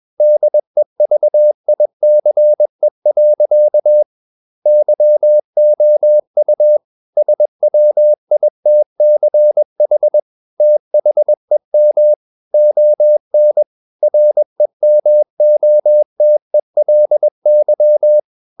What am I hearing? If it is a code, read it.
Morse code, heard as a keyed tone, about 21 wpm, 600 Hz. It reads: DEVICE. YOU SWITCH THEM ON REMOTELY
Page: CW med Gnister